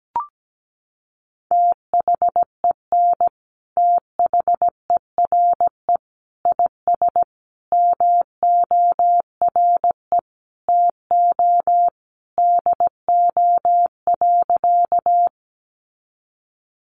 Morse code, heard as a keyed tone, about 17 wpm, 700 Hz. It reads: THEN THERE IS MORE TO DO.
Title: Sentences from Top 100 Words 17wpm